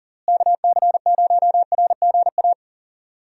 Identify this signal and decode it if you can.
Morse code, heard as a keyed tone, about 40 wpm, 700 Hz. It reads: KC0RGA
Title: USA Call Signs - 6 Characters - Repeated 2x - 40wpm